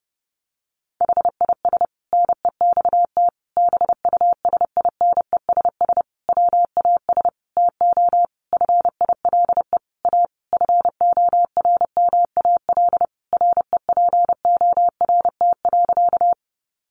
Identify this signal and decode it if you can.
Morse code, heard as a keyed tone, about 30 wpm, 700 Hz. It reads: HIS NEXT BUSINESS WAS TO FILE A FORMAL REPORT.
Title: Sentences from Top 1300 Words 30wpm